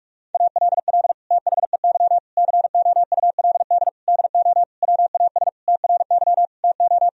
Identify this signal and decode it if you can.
Morse code, heard as a keyed tone, about 46 wpm, 700 Hz. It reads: ALL THEY COULD DO WAS TRY TO
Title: Sentences from Top 1000 Words - Review Part 1 - 45wpm